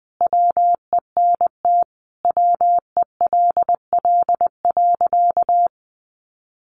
Morse code, heard as a keyed tone, about 20 wpm, 700 Hz. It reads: WENT WELL.
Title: Sentences from Top 800 Words 20wpm